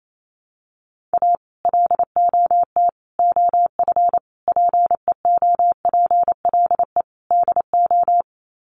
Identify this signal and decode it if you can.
Morse code, heard as a keyed tone, about 28 wpm, 700 Hz. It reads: A LOT OF PEOPLE DO
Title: Sets of 5 Words 28wpm